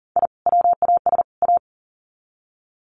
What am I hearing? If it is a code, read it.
Morse code, heard as a keyed tone, about 40 wpm, 700 Hz. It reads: I WAS A
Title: Sets of 3 Words 40wpm